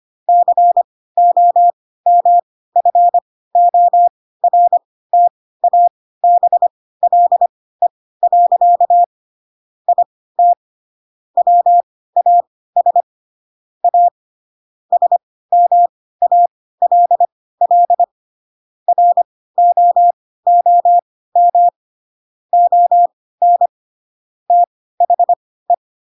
Morse code, heard as a keyed tone, about 25 wpm, spaced as for 16 wpm, 700 Hz. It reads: COMFORTABLE. IT WAS A SMALL ROOM ON THE
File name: Pinocchio / Pinocchio ch3 10 25 ik1hge